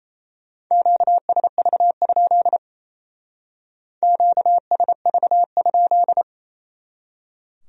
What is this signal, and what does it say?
Morse code, heard as a keyed tone, about 35 wpm, 700 Hz. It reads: QSV? QSV?